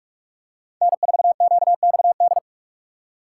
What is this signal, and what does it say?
Morse code, heard as a keyed tone, about 46 wpm, 700 Hz. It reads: N4QXD
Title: USA Call Signs - 5 Characters - 4x Character Spacing - 45wpm